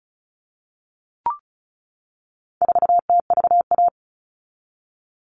Morse code, heard as a keyed tone, about 34 wpm, 700 Hz. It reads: 4TVA
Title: All Cut Numbers - 4-Digit 35wpm